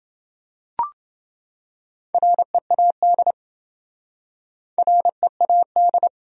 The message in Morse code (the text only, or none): READ READ